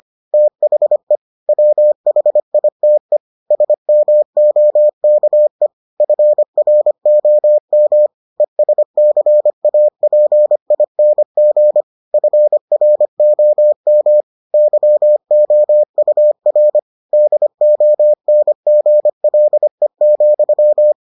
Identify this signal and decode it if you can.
Morse code, heard as a keyed tone, about 25 wpm, 600 Hz. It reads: THE WHITE SMOKE FROM ESCAPING FROM YOUR DONGLE,